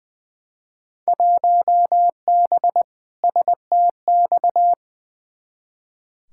Morse code, heard as a keyed tone, about 21 wpm, 700 Hz. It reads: 1B STX
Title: Field Day Exchanges Encore - 20wpm